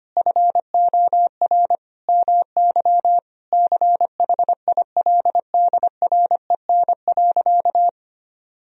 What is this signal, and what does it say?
Morse code, heard as a keyed tone, about 25 wpm, 700 Hz. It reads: FOR MY CHILDREN.
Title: Sentences from Top 1000 Words - Review Part 3 - 25wpm